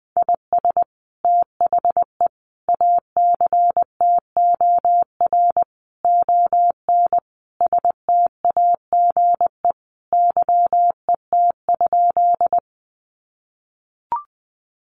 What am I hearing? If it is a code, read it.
Morse code, heard as a keyed tone, about 20 wpm, 700 Hz. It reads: IS THE ACTOR ON STAGE YET?
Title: Sentences from Top 1500 Words 20wpm